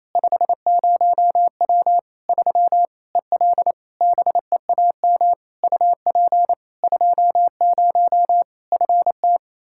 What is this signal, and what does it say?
Morse code, heard as a keyed tone, about 28 wpm, 700 Hz. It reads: 50W 3 EL BEAM UP 20 FT
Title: Practice QSO - Exchange 2 - Info - 28wpm